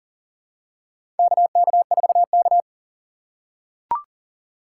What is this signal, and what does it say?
Morse code, heard as a keyed tone, about 38 wpm, 700 Hz. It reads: KK4K
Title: USA Call Signs - 4 Characters - 3x Character Spacing - 40wpm